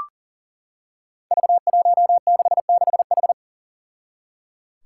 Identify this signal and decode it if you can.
Morse code, heard as a keyed tone, about 38 wpm, 700 Hz. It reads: V166H
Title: Five Challenging Letter-Number Combinations 40wpm